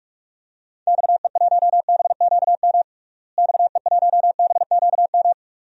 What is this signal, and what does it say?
Morse code, heard as a keyed tone, about 46 wpm, 700 Hz. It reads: XE1BQM XE1BQM
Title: DXCC-100 Call Signs - 6 Characters - Repeated 2x - 45wpm